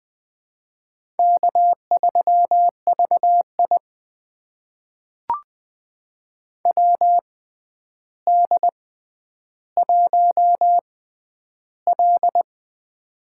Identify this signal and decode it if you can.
Morse code, heard as a keyed tone, about 20 wpm, spaced as for 10 wpm, 700 Hz. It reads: K3VI WD1L
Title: USA Call Signs - 4 Characters - 6x Character Spacing - 20wpm